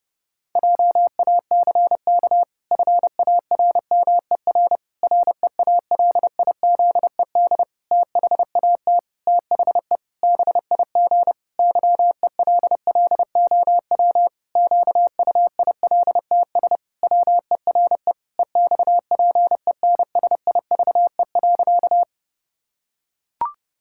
Morse code, heard as a keyed tone, about 30 wpm, 700 Hz. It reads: JACK FARMER REALIZED THAT THE BIG YELLOW QUILTS WERE EXPENSIVE.
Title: Pangram Challenge v2 - 30wpm